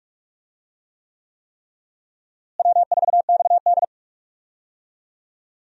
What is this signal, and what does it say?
Morse code, heard as a keyed tone, about 44 wpm, 700 Hz. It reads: W4XD
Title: USA Call Signs - 4 Characters - Repeated 3x - 45wpm